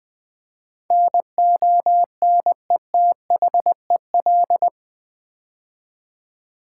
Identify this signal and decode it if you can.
Morse code, heard as a keyed tone, about 20 wpm, 700 Hz. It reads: NONETHEL
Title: Build A Word - 4 Letters At A Time - 11-Letter Words - 20wpm